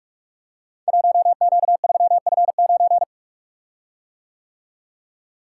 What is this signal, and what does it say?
Morse code, heard as a keyed tone, about 44 wpm, 690 Hz. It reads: 1Q3F9
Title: Code Groups - 5 Letter:Numbers - 45wpm